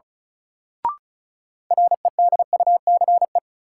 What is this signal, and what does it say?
Morse code, heard as a keyed tone, about 36 wpm, 700 Hz. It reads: REDUCE
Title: Sentences from Top 1100 Words - New Word Drill Alt - 35wpm